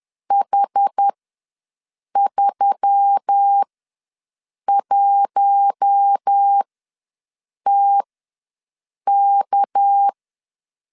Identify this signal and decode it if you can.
Morse code, heard as a keyed tone, about 10 wpm, spaced as for 6 wpm, 800 Hz. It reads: H31TK
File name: LEZIONI / 6 PKX0 / 6 PK ws 10wpm 3745